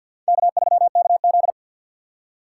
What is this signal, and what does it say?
Morse code, heard as a keyed tone, about 62 wpm, 700 Hz. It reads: K3KZ
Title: Speed Racing - USA Call Signs 50wpm